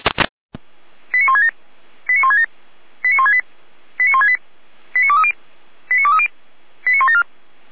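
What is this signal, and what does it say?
Five-tone selective call (ZVEI), heard as quick runs of five tones, tones of about 70 ms. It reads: 89167 89167 89168 89168 89230 89230 89174
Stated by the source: ZVEI five-tone call received by SDR.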